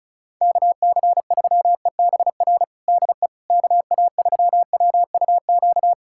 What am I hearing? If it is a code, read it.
Morse code, heard as a keyed tone, about 36 wpm, 700 Hz. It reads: KC3EBR DE KA3WUQ
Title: USA Callsign DE Callsign - 6 Characters - 35wpm